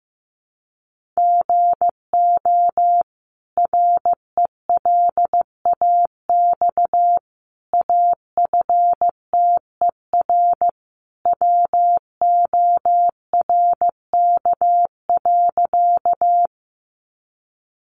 Morse code, heard as a keyed tone, about 15 wpm, 700 Hz. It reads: GO RELAX AFTER WORK.
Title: Sentences from Top 1400 Words 15wpm